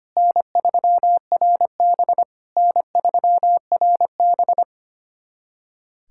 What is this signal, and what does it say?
Morse code, heard as a keyed tone, about 25 wpm, 700 Hz. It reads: N3RB N3RB
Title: SOTA QSO Practice 25wpm